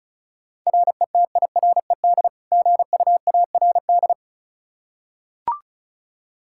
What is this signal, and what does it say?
Morse code, heard as a keyed tone, about 34 wpm, 700 Hz. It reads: RETIRED GUARD
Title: Practice QSO - Exchange 3 - More Info - 35wpm